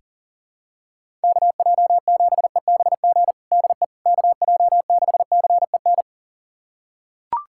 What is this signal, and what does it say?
Morse code, heard as a keyed tone, about 37 wpm, 690 Hz. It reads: KJ7EBG DE KJ6CEN
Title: USA Callsign DE Callsign - 6 Characters - 40wpm